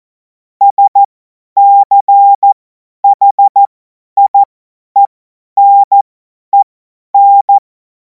Morse code, heard as a keyed tone, about 14 wpm, spaced as for 10 wpm, 800 Hz. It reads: SCHIENEN
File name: Morsekurs / Witze / Witz 10. Februar 2026 50 F70